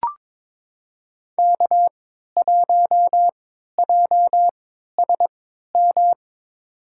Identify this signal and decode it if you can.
Morse code, heard as a keyed tone, about 23 wpm, spaced as for 13 wpm, 700 Hz. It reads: K1JSM
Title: USA Call Signs - 5 Characters - 3x Character Spacing - 22wpm